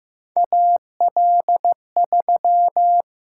Morse code, heard as a keyed tone, about 15 wpm, 700 Hz. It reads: A L 3
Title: USA Call Signs - 6x Character Spacing - 15wpm